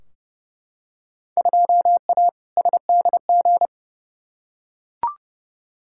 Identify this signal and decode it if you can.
Morse code, heard as a keyed tone, about 30 wpm, 700 Hz. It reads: 2A SDG
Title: Field Day Exchanges Encore - 30wpm